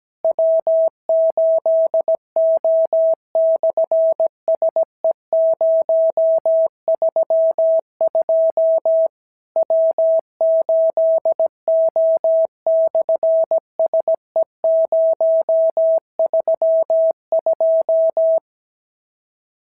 Morse code, heard as a keyed tone, about 17 wpm, 650 Hz. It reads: W8O/SE032 W8O/SE032
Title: SOTA QSO Practice 17wpm